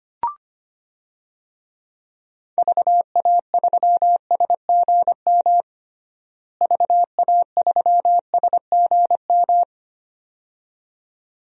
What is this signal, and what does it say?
Morse code, heard as a keyed tone, about 25 wpm, 700 Hz. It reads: VA3SGM VA3SGM
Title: DXCC-100 Call Signs - 6 Characters - Repeated 2x - 25wpm